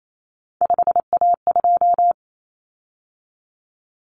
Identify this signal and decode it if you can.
Morse code, heard as a keyed tone, about 28 wpm, 700 Hz. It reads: 5A2
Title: Common Cut Numbers - 3-Digit 28wpm